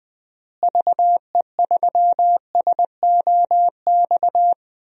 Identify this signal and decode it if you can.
Morse code, heard as a keyed tone, about 19 wpm, 700 Hz. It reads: VE3SOX
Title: DXCC-100 Call Signs - 6 Characters - Repeated 3x - 20wpm